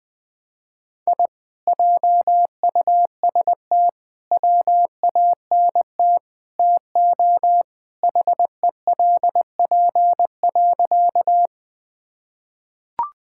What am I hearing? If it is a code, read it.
Morse code, heard as a keyed tone, about 20 wpm, 700 Hz. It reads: I JUST WANT TO HELP.
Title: Sentences with 4-Word Phrases - 20wpm